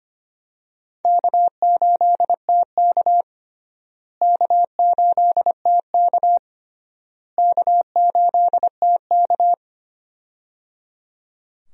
Morse code, heard as a keyed tone, about 25 wpm, 700 Hz. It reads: K8TK K8TK K8TK
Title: USA Call Signs - 4 Characters - Repeated 3x - 25wpm